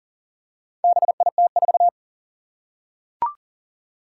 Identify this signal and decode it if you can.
Morse code, heard as a keyed tone, about 40 wpm, 700 Hz. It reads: BIT4
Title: Four Challenging Letter-Number Combinations 40wpm